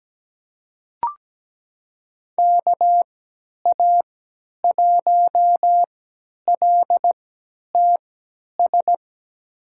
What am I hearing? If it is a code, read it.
Morse code, heard as a keyed tone, about 17 wpm, spaced as for 10 wpm, 700 Hz. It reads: KA1LTS
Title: USA Call Signs - 6 Characters - 3x Character Spacing - 17wpm